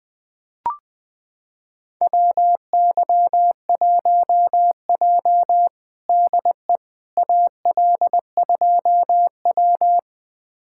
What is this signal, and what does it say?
Morse code, heard as a keyed tone, about 20 wpm, 700 Hz. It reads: WY1J DE AL2W
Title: USA Callsign DE Callsign - 4 Characters - 20wpm